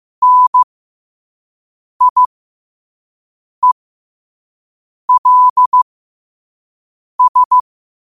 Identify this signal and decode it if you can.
Morse code, heard as a keyed tone, about 15 wpm, spaced as for 5 wpm, 1000 Hz. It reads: NIELS
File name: qotdfiles / 5wpm